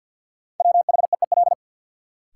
Morse code, heard as a keyed tone, about 53 wpm, 700 Hz. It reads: WHEEL
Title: Sentences from Top 500 Words - New Word Drill - 50wpm